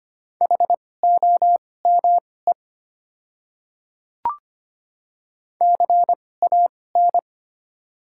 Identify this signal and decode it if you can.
Morse code, heard as a keyed tone, about 25 wpm, spaced as for 18 wpm, 700 Hz. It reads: HOME CAN
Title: 2x Character Spacing - Top 300 Words - 25wpm